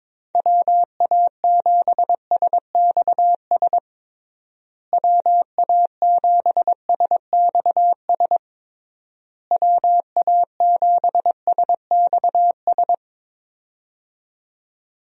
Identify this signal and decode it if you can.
Morse code, heard as a keyed tone, about 22 wpm, 700 Hz. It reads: WA7SXS WA7SXS WA7SXS
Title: USA Call Signs - 6 Characters - Repeated 3x - 22wpm